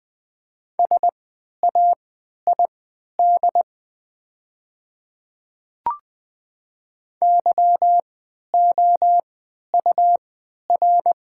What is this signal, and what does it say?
Morse code, heard as a keyed tone, about 20 wpm, spaced as for 11 wpm, 690 Hz. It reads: SAID YOUR
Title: 3x Character Spacing - Top 500 Words - 20wpm